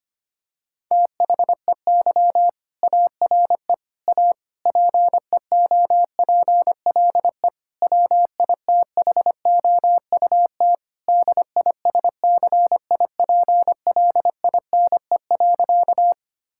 Morse code, heard as a keyed tone, about 25 wpm, 700 Hz. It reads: THEY ARE A PEOPLE WITHOUT DISCIPLINE.
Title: Sentences from Top 1000 Words - Review Part 6 - 25wpm